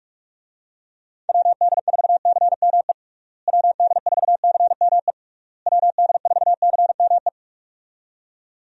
Morse code, heard as a keyed tone, about 46 wpm, 700 Hz. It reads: WD4CME WD4CME WD4CME
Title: USA Call Signs - 6 Characters - Repeated 3x - 45wpm